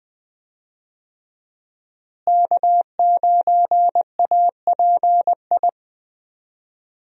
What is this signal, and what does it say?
Morse code, heard as a keyed tone, about 20 wpm, 700 Hz. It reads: K9API
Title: USA Call Signs - 5 Characters - Repeated 3x - 20wpm